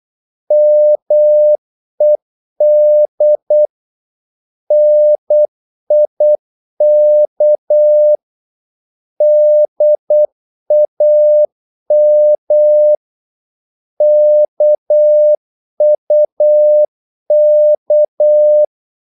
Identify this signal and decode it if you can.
Morse code, heard as a keyed tone, about 8 wpm, 600 Hz. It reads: MED NIK DAM KUK
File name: Korte ord DK 8wpm